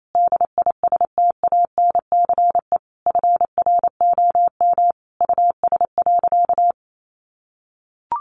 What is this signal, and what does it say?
Morse code, heard as a keyed tone, about 28 wpm, 700 Hz. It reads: DISTANCE FROM US.
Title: Sentences from Top 1200 Words 28wpm